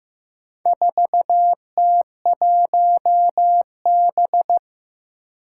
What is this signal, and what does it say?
Morse code, heard as a keyed tone, about 15 wpm, 700 Hz. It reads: 4T1B
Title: Four Challenging Letter-Number Combinations 15wpm